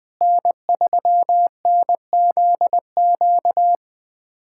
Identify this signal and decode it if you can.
Morse code, heard as a keyed tone, about 20 wpm, 700 Hz. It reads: N3NZQ
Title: USA Callsign DE Callsign 20wpm